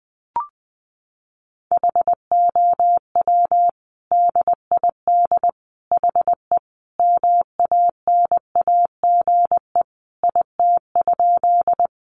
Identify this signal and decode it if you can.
Morse code, heard as a keyed tone, about 20 wpm, 700 Hz. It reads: HOW DID HE MANAGE IT?
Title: Sentences from Top 1400 Words 20wpm